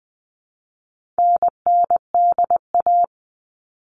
Morse code, heard as a keyed tone, about 20 wpm, 700 Hz. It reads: NNDA
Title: All Cut Numbers - 4-Digit 20wpm